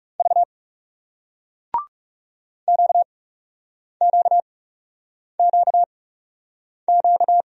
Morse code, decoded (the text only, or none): V QQQQ